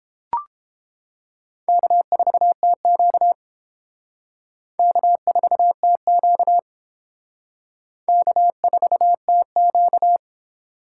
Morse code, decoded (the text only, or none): K4TQ K4TQ K4TQ